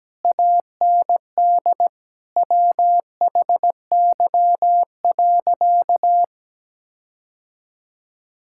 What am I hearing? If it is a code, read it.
Morse code, heard as a keyed tone, about 17 wpm, 700 Hz. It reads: AND WHY.
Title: Sentences from Top 1600 Words 17wpm